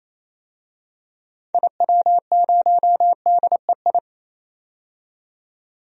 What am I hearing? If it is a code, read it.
Morse code, heard as a keyed tone, about 28 wpm, 700 Hz. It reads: IW0DEI
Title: DXCC-100 Call Signs - 6 Characters - 28wpm